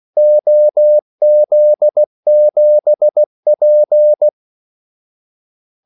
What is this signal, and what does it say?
Morse code, heard as a keyed tone, about 16 wpm, 600 Hz. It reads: OZ7P
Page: Kaldesignaler 16 wpm